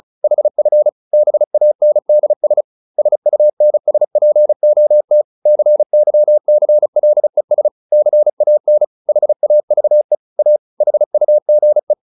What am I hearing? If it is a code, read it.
Morse code, heard as a keyed tone, about 34 wpm, 600 Hz. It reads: HF BANDS SUNSPOT CYCLES CAN HAVE A HUGE